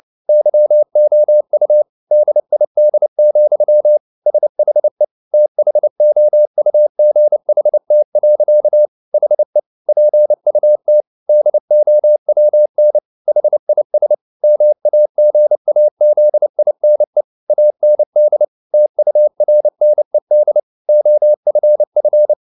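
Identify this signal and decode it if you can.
Morse code, heard as a keyed tone, about 29 wpm, 600 Hz. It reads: YOU DID, SHE THOUGHT. HE PUT DOWN HIS MAGAZINE AND TURNED OFF
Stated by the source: Morse code at 29 wpm, sending Ken Follett's novel Never in English, from CW med Gnister.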